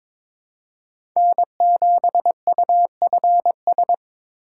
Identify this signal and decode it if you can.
Morse code, heard as a keyed tone, about 22 wpm, 700 Hz. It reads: N7UFS
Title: USA Call Signs - 5 Characters - 3x Character Spacing - 22wpm